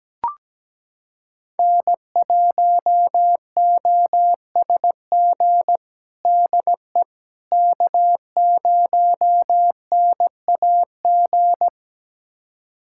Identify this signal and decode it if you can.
Morse code, heard as a keyed tone, about 17 wpm, 700 Hz. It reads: N1OSG DE K0NAG
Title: USA Callsign DE Callsign - 5 Characters - 17wpm